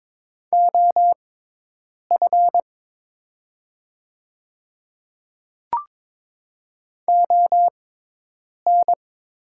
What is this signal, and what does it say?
Morse code, heard as a keyed tone, about 22 wpm, spaced as for 8 wpm, 690 Hz. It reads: OF ON
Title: 6x Character Spacing - Top 100 Words - 22wpm